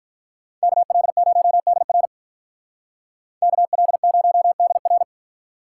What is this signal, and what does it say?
Morse code, heard as a keyed tone, about 46 wpm, 700 Hz. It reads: KL0DR KL0DR